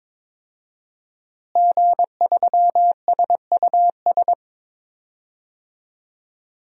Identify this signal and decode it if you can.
Morse code, heard as a keyed tone, about 22 wpm, 700 Hz. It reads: G3SUS
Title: DXCC-100 Call Signs - 5 Characters - Repeated 2x - 22wpm